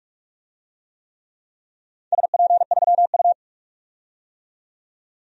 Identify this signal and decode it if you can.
Morse code, heard as a keyed tone, about 45 wpm, 700 Hz. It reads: SP3U
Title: DXCC-100 Call Signs - 4 Characters - 45wpm